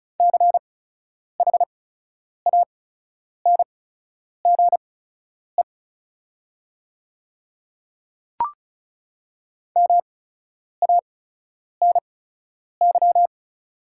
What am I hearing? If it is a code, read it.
Morse code, heard as a keyed tone, about 36 wpm, spaced as for 10 wpm, 690 Hz. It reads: CHANGE MANY
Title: 8x Character Spacing - Top 200 Words - 35wpm